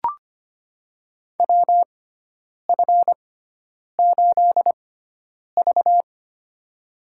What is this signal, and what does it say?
Morse code, heard as a keyed tone, about 25 wpm, spaced as for 9 wpm, 700 Hz. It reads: WF8V
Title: USA Call Signs - 4 Characters - 6x Character Spacing - 25wpm